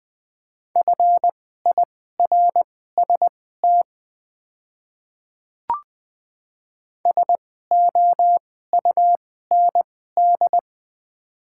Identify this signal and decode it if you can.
Morse code, heard as a keyed tone, about 21 wpm, spaced as for 15 wpm, 700 Hz. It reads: FIRST SOUND
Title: 2x Character Spacing - Top 100 Words - 20wpm